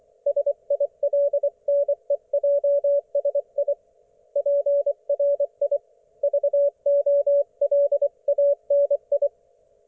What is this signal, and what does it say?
Morse code, heard as a keyed tone, about 24 wpm, 570 Hz. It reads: SILNEJSI PRI VOLANI
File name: Archiv / CUC224 / ok1fhi cul224